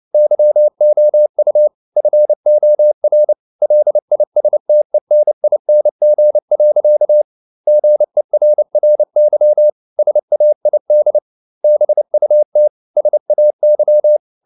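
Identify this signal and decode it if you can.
Morse code, heard as a keyed tone, about 29 wpm, 600 Hz. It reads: YOU FOR LISTENING. GERRY SAID BUT SAY
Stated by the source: Morse code at 29 wpm, sending Ken Follett's novel Never in English, from CW med Gnister.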